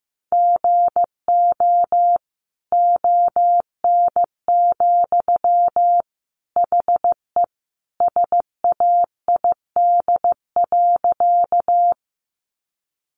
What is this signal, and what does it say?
Morse code, heard as a keyed tone, about 15 wpm, 700 Hz. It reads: GO ON, HE SAID.
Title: Sentences from Top 100 Words 15wpm